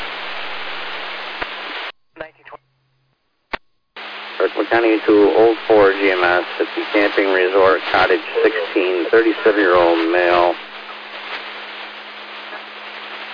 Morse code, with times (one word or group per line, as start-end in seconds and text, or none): none